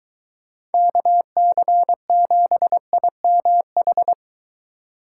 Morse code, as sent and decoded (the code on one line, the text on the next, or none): -.- -.-. --... .. -- ....
KC7IMH